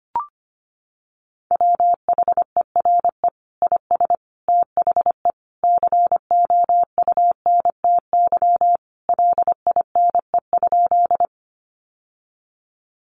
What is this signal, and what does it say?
Morse code, heard as a keyed tone, about 25 wpm, 700 Hz. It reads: WHERE IS THE COUNTY LINE?
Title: Sentences from Top 1200 Words 25wpm